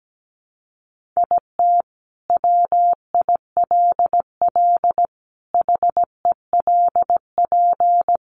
Morse code, heard as a keyed tone, about 17 wpm, 700 Hz. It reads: IT WILL HELP
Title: Sentences from Top 1400 Words 17wpm